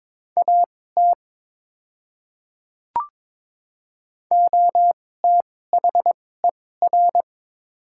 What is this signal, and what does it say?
Morse code, heard as a keyed tone, about 22 wpm, spaced as for 16 wpm, 700 Hz. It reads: AT OTHER
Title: 2x Character Spacing - Top 100 Words - 22wpm